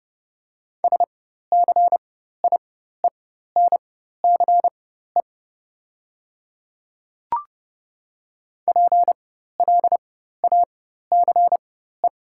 Morse code, heard as a keyed tone, about 30 wpm, spaced as for 14 wpm, 700 Hz. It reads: SCIENCE PLACE
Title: 4x Character Spacing - Top 500 Words - 30wpm